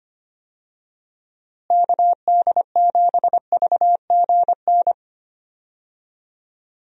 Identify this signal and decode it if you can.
Morse code, heard as a keyed tone, about 25 wpm, 700 Hz. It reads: KD7VGN
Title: USA Call Signs - 6 Characters - Repeated 2x - 25wpm